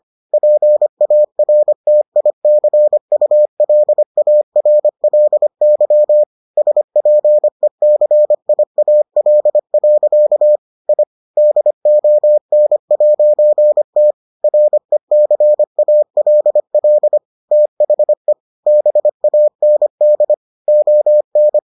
Morse code, heard as a keyed tone, about 25 wpm, 600 Hz. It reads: PARTICULARLY SPECIAL. I DON'T RECALL THE BAND ON